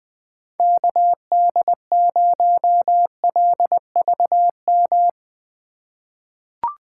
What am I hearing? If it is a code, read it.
Morse code, heard as a keyed tone, about 20 wpm, 700 Hz. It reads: KD0LVM
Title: USA Call Signs - 6 Characters - 4x Character Spacing - 20wpm